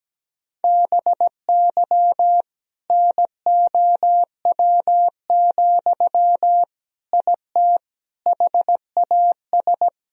Morse code, heard as a keyed tone, about 17 wpm, 700 Hz. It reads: BY NOW, IT HAS
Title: Sentences from Top 1000 Words - Review Part 1 - 17wpm